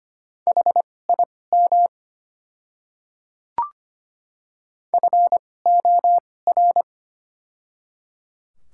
Morse code, heard as a keyed tone, about 25 wpm, spaced as for 18 wpm, 700 Hz. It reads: HIM FOR
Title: 2x Character Spacing - Top 100 Words - 25wpm